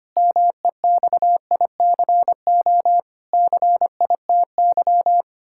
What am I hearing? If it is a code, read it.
Morse code, heard as a keyed tone, about 25 wpm, 700 Hz. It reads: MEXICO CITY
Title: Pangram Challenge v2 - 25wpm